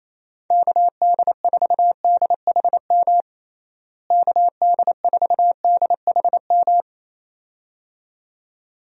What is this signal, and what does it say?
Morse code, heard as a keyed tone, about 28 wpm, 700 Hz. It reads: KD4DHM KD4DHM
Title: USA Call Signs - 6 Characters - Repeated 2x - 28wpm